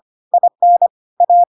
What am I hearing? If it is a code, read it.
Morse code, heard as a keyed tone, about 25 wpm, 700 Hz. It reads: IN A
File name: Pinocchio / Pinocchio ch5 25 25 ik1hge